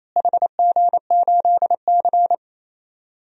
Morse code, heard as a keyed tone, about 28 wpm, 700 Hz. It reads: HG8C
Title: DXCC-100 Call Signs - 4 Characters - Repeated 3x - 28wpm